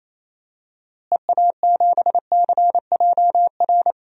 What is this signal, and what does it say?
Morse code, heard as a keyed tone, about 28 wpm, 700 Hz. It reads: EA7CJR
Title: DXCC-100 Call Signs - 6 Characters - Repeated 3x - 28wpm